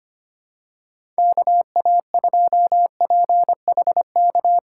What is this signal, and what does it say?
Morse code, heard as a keyed tone, about 25 wpm, 700 Hz. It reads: KA2PHK
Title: USA Call Signs - 6 Characters - 3x Character Spacing - 25wpm